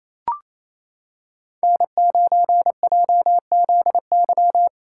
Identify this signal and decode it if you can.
Morse code, heard as a keyed tone, about 28 wpm, 700 Hz. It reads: N9JZY
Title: K1USN SST QSO Practice - 28wpm